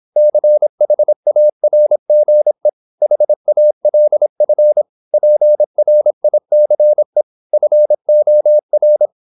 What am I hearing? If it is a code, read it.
Morse code, heard as a keyed tone, about 26 wpm, 600 Hz. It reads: CHARGE HALF PRICE FOR
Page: Never 26wpm | CW med Gnister